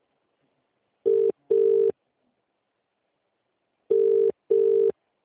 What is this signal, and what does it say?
Ringback tone (UK), 400 + 450 Hz.